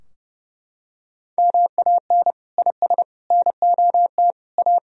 Morse code, heard as a keyed tone, about 32 wpm, 700 Hz. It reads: MAN IS NOT A
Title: Sentences from Top 1000 Words - Review Part 2 - 30wpm